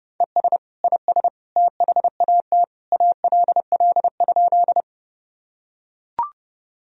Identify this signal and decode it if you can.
Morse code, heard as a keyed tone, about 32 wpm, 700 Hz. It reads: ES IS THAT ALL?
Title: Sentences from Top 100 Words and Abbreviations 30wpm